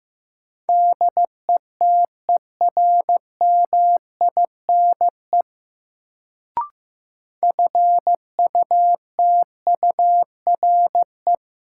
Morse code, heard as a keyed tone, about 15 wpm, 700 Hz. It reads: DETERMINE FUTURE
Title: Sentences from Top 700 Words - New Word Drill - 15wpm